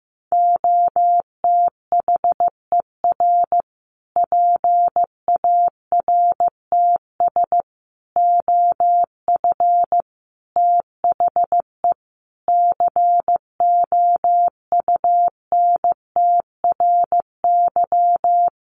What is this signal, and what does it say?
Morse code, heard as a keyed tone, about 15 wpm, 700 Hz. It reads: OTHER PARTS OF THE COUNTRY
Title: Sets of 5 Words Encore 15wpm